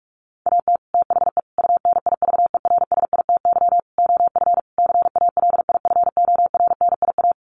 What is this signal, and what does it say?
Morse code, heard as a keyed tone, about 44 wpm, 700 Hz. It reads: AT THE UNIVERSITY OF CALIFORNIA
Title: Sets of 5 Words Encore2 45wpm